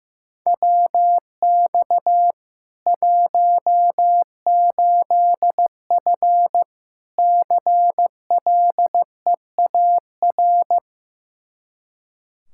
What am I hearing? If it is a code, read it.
Morse code, heard as a keyed tone, about 15 wpm, 700 Hz. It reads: WX 18F CLEAR
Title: Practice QSO - Exchange 2 - Info - 15wpm